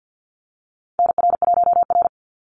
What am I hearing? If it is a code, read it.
Morse code, heard as a keyed tone, about 50 wpm, 700 Hz. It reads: NR1R
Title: Call Signs Repeated 2x - 50wpm